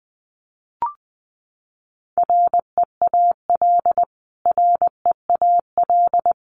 Morse code, heard as a keyed tone, about 20 wpm, 700 Hz. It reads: REAL REAL
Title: Top 300 Words Repeated 20wpm